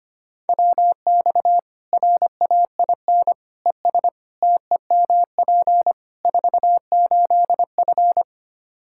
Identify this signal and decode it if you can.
Morse code, heard as a keyed tone, about 25 wpm, 700 Hz. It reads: WX RAIN ES TEMP 48F
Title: Practice QSO - Exchange 2 - Info - 25wpm